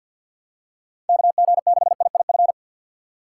Morse code, heard as a keyed tone, about 49 wpm, 700 Hz. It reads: KG6IIF
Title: USA Call Signs - 6 Characters - 4x Character Spacing - 50wpm